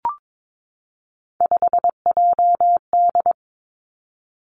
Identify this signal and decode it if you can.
Morse code, heard as a keyed tone, about 22 wpm, 700 Hz. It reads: 5JD
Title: Three Challenging Letter-Number Combinations 22wpm